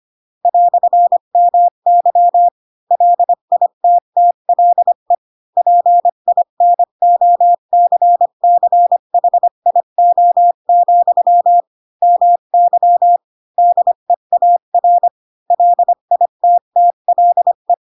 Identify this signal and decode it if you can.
Morse code, heard as a keyed tone, about 25 wpm, 700 Hz. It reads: "MY LITTLE PINOCCHIO, MY DEAR LITTLE
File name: Pinocchio / Pinocchio ch7 20 25 ik1hge